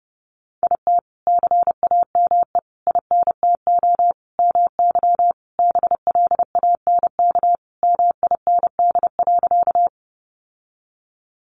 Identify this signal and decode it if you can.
Morse code, heard as a keyed tone, about 30 wpm, 700 Hz. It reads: IT CAME INTO MY BLANK MIND.
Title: Sentences from Top 1500 Words 30wpm